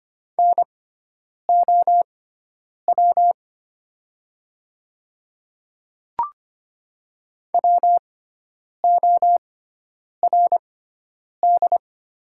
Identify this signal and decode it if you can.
Morse code, heard as a keyed tone, about 25 wpm, spaced as for 9 wpm, 700 Hz. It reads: NOW WORD
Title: 6x Character Spacing - Top 100 Words - 25wpm